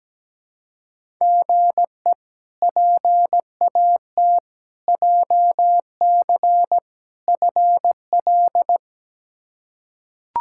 Morse code, heard as a keyed tone, about 17 wpm, 700 Hz. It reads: GE PAT JC FL
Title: K1USN SST QSO Practice - 17wpm